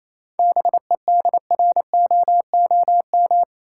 Morse code, heard as a keyed tone, about 28 wpm, 700 Hz. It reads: BEDROOM
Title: Seven Letter Words - 28wpm